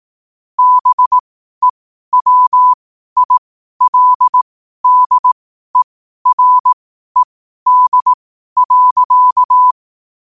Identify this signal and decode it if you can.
Morse code, heard as a keyed tone, about 18 wpm, spaced as for 12 wpm, 1000 Hz. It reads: BEWILDERED.
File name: qotdfiles / 12wpm